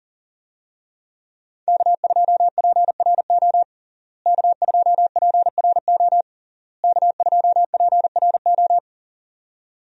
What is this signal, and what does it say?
Morse code, heard as a keyed tone, about 40 wpm, 700 Hz. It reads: K2PRO K2PRO K2PRO
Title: USA Call Signs - 5 Characters - Repeated 3x - 40wpm